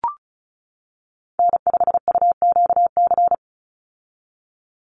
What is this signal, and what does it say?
Morse code, heard as a keyed tone, about 34 wpm, 700 Hz. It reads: N5UQC
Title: USA Call Signs 35wpm